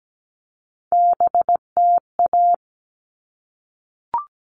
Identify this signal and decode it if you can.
Morse code, heard as a keyed tone, about 17 wpm, 700 Hz. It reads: BTA
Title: All Cut Numbers - 3-Digit 17wpm